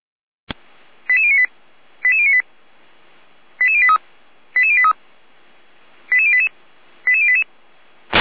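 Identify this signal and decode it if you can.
Five-tone selective call (ZVEI), heard as quick runs of five tones, tones of about 70 ms. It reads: 88098 88098 88083 88083 88088 88088
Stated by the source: ZVEI five-tone selective call received by SDR.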